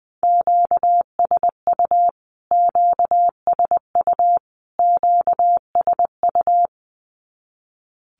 Morse code, heard as a keyed tone, about 20 wpm, 700 Hz. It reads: QSU QSU QSU
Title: Q-Codes Repeated 3x 20wpm